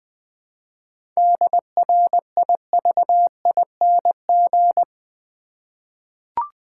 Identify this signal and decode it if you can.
Morse code, heard as a keyed tone, about 20 wpm, 700 Hz. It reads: DRIVING
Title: 7 Letter Words - First 3 Letters Given - 20wpm